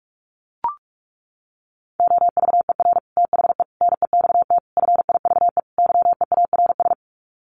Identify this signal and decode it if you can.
Morse code, heard as a keyed tone, about 44 wpm, 700 Hz. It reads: OVER THE NEXT FIVE YEARS
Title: Sets of 5 Words 45wpm